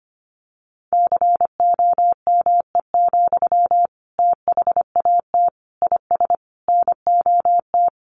Morse code, heard as a keyed tone, about 25 wpm, 700 Hz. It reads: COME, THAT IS NOT
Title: Sentences from Top 700 Words 25wpm